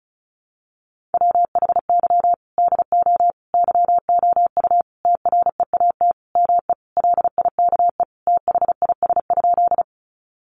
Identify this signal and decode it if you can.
Morse code, heard as a keyed tone, about 34 wpm, 700 Hz. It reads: WHY DO YOU TREAT ME LIKE THIS?
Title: Sentences from Top 1300 Words 35wpm